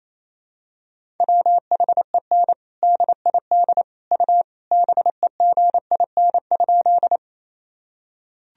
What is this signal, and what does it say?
Morse code, heard as a keyed tone, about 28 wpm, 700 Hz. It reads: WHEN DID U BEGIN?
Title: Sentences from Top 300 Words and Abbreviations 28wpm